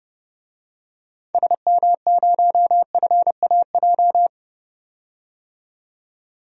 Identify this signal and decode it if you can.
Morse code, heard as a keyed tone, about 32 wpm, 700 Hz. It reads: SM0FAJ
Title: DXCC-100 Call Signs - 6 Characters - 30wpm